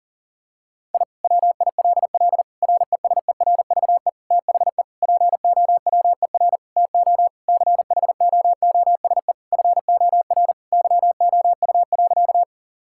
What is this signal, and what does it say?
Morse code, heard as a keyed tone, about 40 wpm, 700 Hz. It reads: I WILL RESERVE THE POWER TO CHOOSE FOR YOU.
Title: Sentences from Top 1000 Words - Review Part 6 - 40wpm